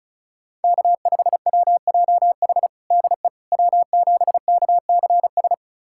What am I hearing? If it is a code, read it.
Morse code, heard as a keyed tone, about 34 wpm, 700 Hz. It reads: K5WJH DE W7KCS
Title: USA Callsign DE Callsign - 5 Characters - 35wpm